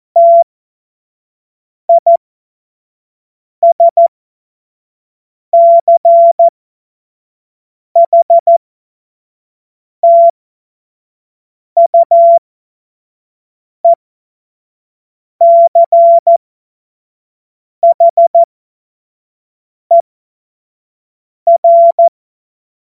Morse code, heard as a keyed tone, about 14 wpm, spaced as for 5 wpm, 680 Hz. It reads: TISCHTUECHER
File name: Morsekurs / Witze / Witz 29. Juni 2021, 25 F70